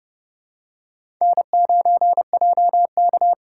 Morse code, heard as a keyed tone, about 28 wpm, 700 Hz. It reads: N9JK
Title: USA Call Signs - 4 Characters - 3x Character Spacing - 30wpm